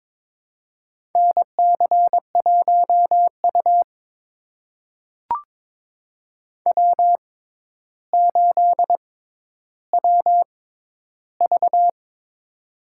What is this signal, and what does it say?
Morse code, heard as a keyed tone, about 22 wpm, spaced as for 11 wpm, 700 Hz. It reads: NC1U W8WV
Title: USA Call Signs - 4 Characters - 6x Character Spacing - 22wpm